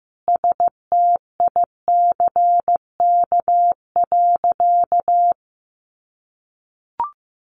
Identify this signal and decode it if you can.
Morse code, heard as a keyed tone, about 15 wpm, 700 Hz. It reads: STICK.
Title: Sentences from Top 1200 Words 15wpm